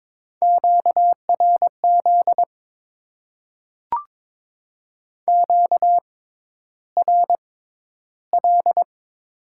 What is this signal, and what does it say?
Morse code, heard as a keyed tone, about 21 wpm, spaced as for 11 wpm, 700 Hz. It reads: QRZ QRL
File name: Top 90 QSO Elements - 6x Character Spacing - 22wpm